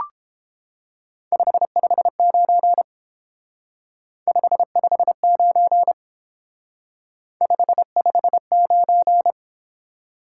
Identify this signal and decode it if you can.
Morse code, heard as a keyed tone, about 29 wpm, 700 Hz. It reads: 559 559 559